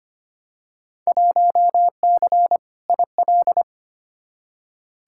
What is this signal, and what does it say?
Morse code, heard as a keyed tone, about 25 wpm, 700 Hz. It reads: 1C IL
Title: Field Day Exchanges - 25wpm